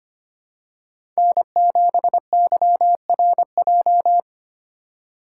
Morse code, heard as a keyed tone, about 25 wpm, 700 Hz. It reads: N7YRJ
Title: USA Call Signs - 5 Characters - 3x Character Spacing - 25wpm